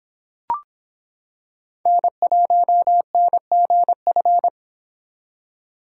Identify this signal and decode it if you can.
Morse code, heard as a keyed tone, about 26 wpm, 700 Hz. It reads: N1NGF